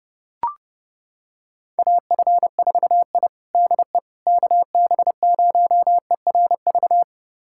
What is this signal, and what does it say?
Morse code, heard as a keyed tone, about 32 wpm, 700 Hz. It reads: AF4I DE KB0ERV
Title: USA Callsign DE Callsign 30wpm